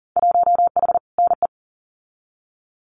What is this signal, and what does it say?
Morse code, heard as a keyed tone, about 41 wpm, 700 Hz. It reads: 1H NE
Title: Winter Field Day Exchanges Encore 40wpm